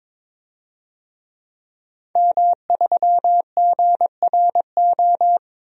Morse code, heard as a keyed tone, about 22 wpm, 700 Hz. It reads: M3GRO
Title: DXCC-100 Call Signs - 5 Characters - 22wpm